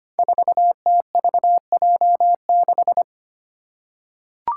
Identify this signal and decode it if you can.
Morse code, heard as a keyed tone, about 25 wpm, 700 Hz. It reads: 4TVJ6
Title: Five Challenging Letter-Number Combinations 25wpm